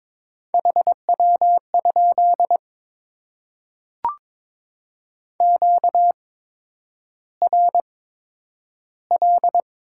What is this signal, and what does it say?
Morse code, heard as a keyed tone, about 22 wpm, spaced as for 9 wpm, 700 Hz. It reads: HW? QRL
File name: Top 20 QSO Elements - 8x Character Spacing - 22wpm